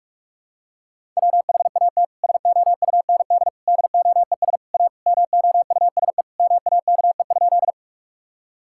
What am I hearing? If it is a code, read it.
Morse code, heard as a keyed tone, about 46 wpm, 700 Hz. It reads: WHAT SOUND DOES A MOUSE MAKE?
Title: Sentences from Top 1000 Words - Review Part 5 - 45wpm